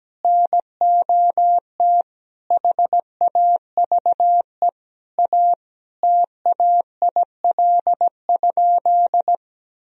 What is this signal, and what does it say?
Morse code, heard as a keyed tone, about 17 wpm, 700 Hz. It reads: NOT HAVE A TAIL?
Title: Sentences from Top 1000 Words - Review Part 3 - 17wpm